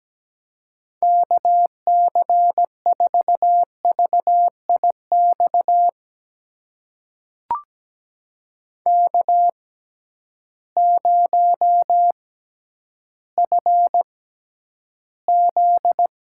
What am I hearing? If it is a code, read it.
Morse code, heard as a keyed tone, about 17 wpm, spaced as for 10 wpm, 700 Hz. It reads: KC4VIX K0FZ
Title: USA Call Signs - 6x Character Spacing - 17wpm